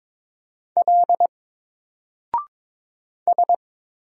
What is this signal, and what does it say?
Morse code, heard as a keyed tone, about 22 wpm, 700 Hz. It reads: L S
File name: Single-Letter-Number-22wpm